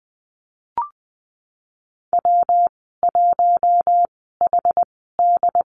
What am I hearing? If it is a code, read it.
Morse code, heard as a keyed tone, about 20 wpm, spaced as for 14 wpm, 700 Hz. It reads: W1HD
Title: 122 Lesson 20g - Number 3 - Call Signs 20wpm